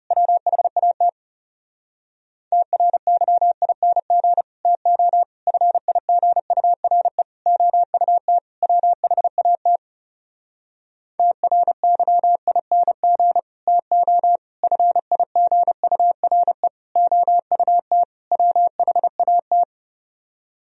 Morse code, heard as a keyed tone, about 34 wpm, 700 Hz. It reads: WHAT TRYING TO FIGURE OUT WHAT TRYING TO FIGURE OUT WHAT
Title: Speed-Racing - Sets of 5 Words 30wpm